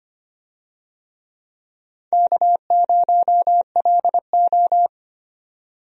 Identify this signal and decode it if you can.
Morse code, heard as a keyed tone, about 25 wpm, 700 Hz. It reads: K0LO
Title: USA Call Signs - 4 Characters - 25wpm